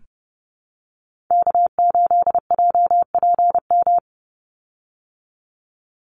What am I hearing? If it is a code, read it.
Morse code, heard as a keyed tone, about 30 wpm, 700 Hz. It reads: K8JPM
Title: Call Signs Repeated 3x - 30wpm